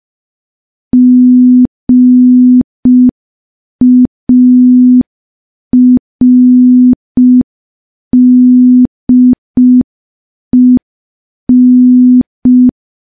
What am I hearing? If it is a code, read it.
Morse code, heard as a keyed tone, about 5 wpm, 250 Hz. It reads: GARDEN
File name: morse / morse-training-05WPM